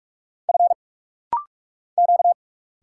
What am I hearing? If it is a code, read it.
Morse code, heard as a keyed tone, about 46 wpm, 700 Hz. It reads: F Q
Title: Single Letters - Rapid-Fire 45wpm